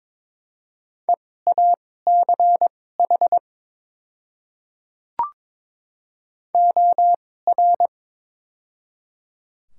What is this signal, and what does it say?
Morse code, heard as a keyed tone, about 22 wpm, spaced as for 16 wpm, 700 Hz. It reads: EACH OR
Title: 2x Character Spacing - Top 100 Words - 22wpm